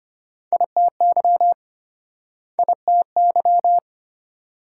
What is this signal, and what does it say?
Morse code, heard as a keyed tone, about 34 wpm, 700 Hz. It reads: ITY ITY